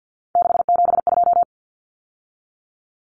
Morse code, heard as a keyed tone, about 50 wpm, 700 Hz. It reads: 672
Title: Numbers - 3-Digit 50wpm